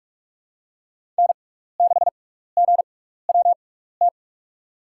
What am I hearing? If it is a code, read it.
Morse code, heard as a keyed tone, about 46 wpm, spaced as for 16 wpm, 700 Hz. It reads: N6GWT
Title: USA Call Signs - 5 Characters - 6x Character Spacing - 45wpm